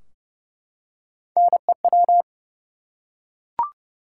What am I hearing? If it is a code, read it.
Morse code, heard as a keyed tone, about 29 wpm, 700 Hz. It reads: NEW
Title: Three Letter Words - 30wpm